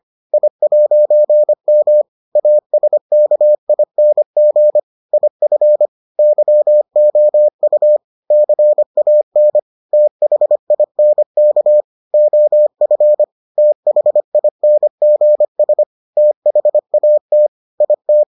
Morse code, heard as a keyed tone, about 25 wpm, 600 Hz. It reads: I'M ASKING IF YOU CAN THINK OF THINGS THAT IT